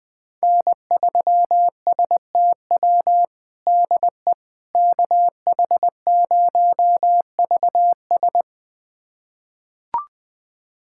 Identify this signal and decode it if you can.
Morse code, heard as a keyed tone, about 21 wpm, 700 Hz. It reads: N3STW DE KH0VS
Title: USA Callsign DE Callsign - 5 Characters - 20wpm